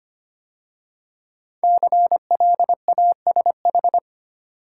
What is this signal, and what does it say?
Morse code, heard as a keyed tone, about 25 wpm, 700 Hz. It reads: CLASH
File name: 5 Letter Words - First 3 Letters Given - 25wpm